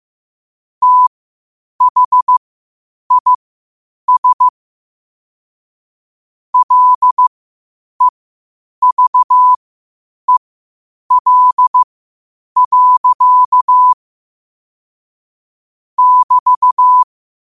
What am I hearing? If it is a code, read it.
Morse code, heard as a keyed tone, about 15 wpm, spaced as for 8 wpm, 1000 Hz. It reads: THIS LEVEL. <BT>
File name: qotdfiles / 8wpm